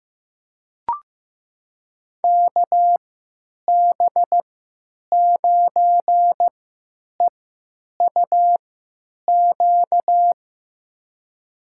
Morse code, heard as a keyed tone, about 15 wpm, spaced as for 9 wpm, 700 Hz. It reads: KB9EUQ
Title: USA Call Signs - 6 Characters - 3x Character Spacing - 15wpm